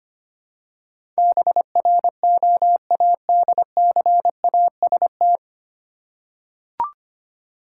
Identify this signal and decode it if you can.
Morse code, heard as a keyed tone, about 25 wpm, 700 Hz. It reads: BROADCAST
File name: Nine Letter Words - 25wpm